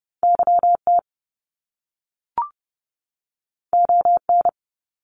Morse code, heard as a keyed tone, about 30 wpm, 700 Hz. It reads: YT ON
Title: Canadian Provinces and Territories 30wpm